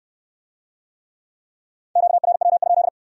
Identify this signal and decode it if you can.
Morse code, heard as a keyed tone, about 64 wpm, 700 Hz. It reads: QRW?